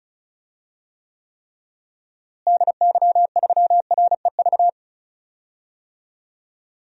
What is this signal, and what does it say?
Morse code, heard as a keyed tone, about 34 wpm, 700 Hz. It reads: DY3REV
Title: DXCC-100 Call Signs - 6 Characters - Repeated 3x - 35wpm